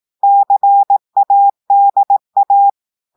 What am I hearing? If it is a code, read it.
Morse code, heard as a keyed tone, about 18 wpm, 800 Hz. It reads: CADA
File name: audios / portugais / portugais texte en clair 1 vitesse 18 mots minutes